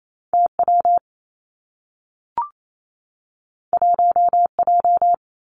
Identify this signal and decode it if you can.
Morse code, heard as a keyed tone, about 28 wpm, 700 Hz. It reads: T W 1 J
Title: Two Challenging Letter-Number Combinations 28wpm